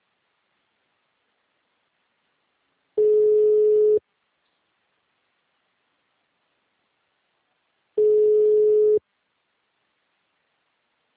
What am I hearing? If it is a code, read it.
Ringback tone (Europe, 425 Hz).